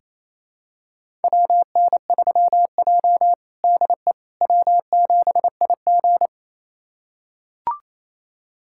Morse code, heard as a keyed tone, about 28 wpm, 700 Hz. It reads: WN3J DE W7IG
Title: USA Callsign DE Callsign - 4 Characters - 28wpm